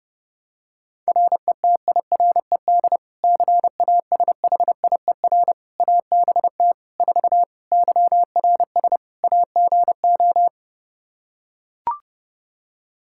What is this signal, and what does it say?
Morse code, heard as a keyed tone, about 30 wpm, 700 Hz. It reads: RETIRED CASHIER ABT 4 YRS AGO
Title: Practice QSO - Exchange 3 - More Info - 30wpm